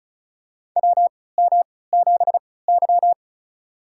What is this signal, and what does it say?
Morse code, heard as a keyed tone, about 36 wpm, spaced as for 20 wpm, 700 Hz. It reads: WM7Y
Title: USA Call Signs - 4 Characters - 3x Character Spacing - 35wpm